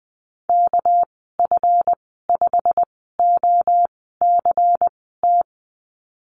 Morse code, heard as a keyed tone, about 20 wpm, spaced as for 14 wpm, 700 Hz. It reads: KF5OCT
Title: 231 Lesson 37g - Letter x - Call Signs 20wpm